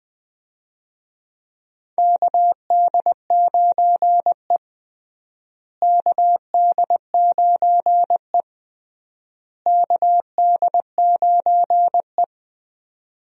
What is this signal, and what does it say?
Morse code, heard as a keyed tone, about 20 wpm, 700 Hz. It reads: KD9E KD9E KD9E
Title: USA Call Signs - 4 Characters - Repeated 3x - 20wpm